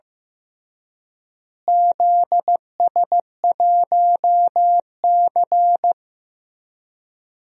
Morse code, heard as a keyed tone, about 15 wpm, 700 Hz. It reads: ZS1C
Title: DXCC-100 Call Signs - 4 Characters - Repeated 3x - 15wpm